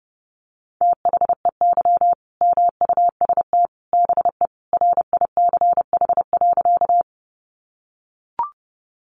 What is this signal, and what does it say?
Morse code, heard as a keyed tone, about 30 wpm, 700 Hz. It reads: THEY MUST BE RICH.
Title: Sentences from Top 1100 Words 30wpm